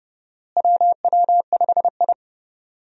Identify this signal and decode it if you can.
Morse code, heard as a keyed tone, about 30 wpm, 700 Hz. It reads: WW5I
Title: USA Call Signs - 4 Characters - Repeated 3x - 30wpm